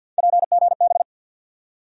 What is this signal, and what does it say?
Morse code, heard as a keyed tone, about 48 wpm, 700 Hz. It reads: PGB
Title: USA Call Sign Postfixes - 3 Letters 50wpm